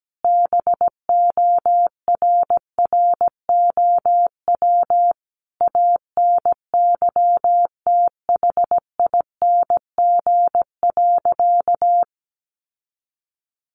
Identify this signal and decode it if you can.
Morse code, heard as a keyed tone, about 17 wpm, 700 Hz. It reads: BORROW ANYTHING.
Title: Sentences from Top 1500 Words 17wpm